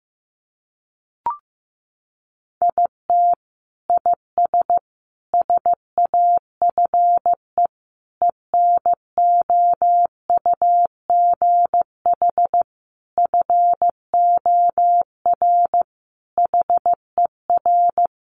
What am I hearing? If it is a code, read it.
Morse code, heard as a keyed tone, about 15 wpm, 700 Hz. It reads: IT IS SAFE ENOUGH FOR HER
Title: Sentences from Top 900 Words 15wpm